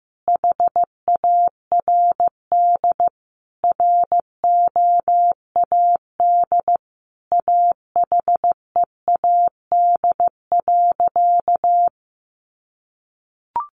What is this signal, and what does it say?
Morse code, heard as a keyed tone, about 15 wpm, 700 Hz. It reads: HARD ROAD AHEAD.
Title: Sentences from Top 1100 Words 15wpm